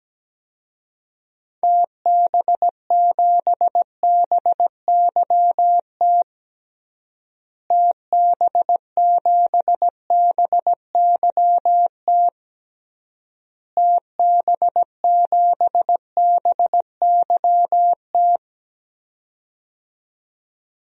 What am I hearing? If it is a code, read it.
Morse code, heard as a keyed tone, about 17 wpm, 700 Hz. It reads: TB7BYT TB7BYT TB7BYT
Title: DXCC-100 Call Signs - 6 Characters - Repeated 3x - 17wpm